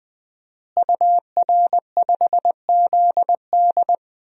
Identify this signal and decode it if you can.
Morse code, heard as a keyed tone, about 20 wpm, 700 Hz. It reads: UR5ZD
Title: DXCC-100 Call Signs - 5 Characters - Repeated 2x - 20wpm